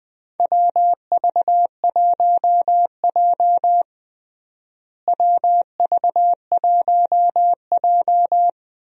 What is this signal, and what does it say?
Morse code, heard as a keyed tone, about 20 wpm, 700 Hz. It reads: WV1J WV1J
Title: USA Call Signs - 4 Characters - Repeated 2x - 20wpm